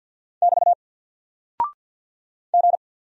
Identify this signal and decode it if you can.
Morse code, heard as a keyed tone, about 48 wpm, 700 Hz. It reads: <BT> G
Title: Single Character (Letter:Number:Punctuation:Prosign) Alt 50wpm